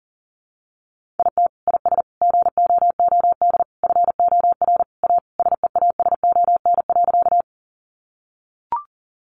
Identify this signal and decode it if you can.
Morse code, heard as a keyed tone, about 40 wpm, 700 Hz. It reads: IT IS GOOD FOR A SEASON.
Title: Sentences from Top 1400 Words 40wpm